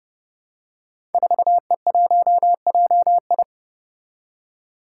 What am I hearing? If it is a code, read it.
Morse code, heard as a keyed tone, about 29 wpm, 700 Hz. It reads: 4E1JI
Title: Five Challenging Letter-Number Combinations 30wpm